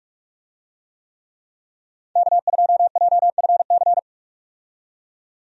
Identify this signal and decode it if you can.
Morse code, heard as a keyed tone, about 46 wpm, 700 Hz. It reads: K2JFC
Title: USA Call Signs - 5 Characters - 45wpm